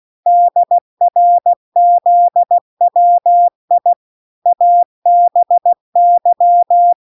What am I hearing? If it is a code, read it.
Morse code, heard as a keyed tone, about 16 wpm, 700 Hz. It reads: DRZWI ABY